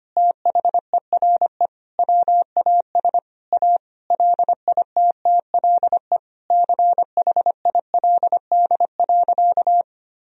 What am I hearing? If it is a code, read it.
Morse code, heard as a keyed tone, about 25 wpm, 700 Hz. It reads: THERE WAS A LITTLE CHILD.
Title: Sentences from Top 1000 Words - Review Part 1 - 25wpm